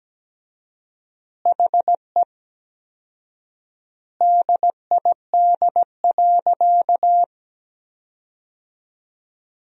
Morse code, heard as a keyed tone, about 17 wpm, 700 Hz. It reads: HE DID.
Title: 4x Word-Spacing - Sentences from Top 500 Words - 17wpm